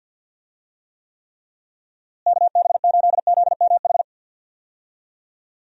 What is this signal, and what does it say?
Morse code, heard as a keyed tone, about 51 wpm, 700 Hz. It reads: KB8ZMH
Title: USA Call Signs - 6 Characters - Repeated 3x - 50wpm